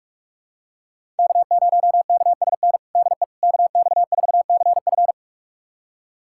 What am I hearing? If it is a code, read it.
Morse code, heard as a keyed tone, about 41 wpm, 700 Hz. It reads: K0KSN DE KX4CF
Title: USA Callsign DE Callsign 45wpm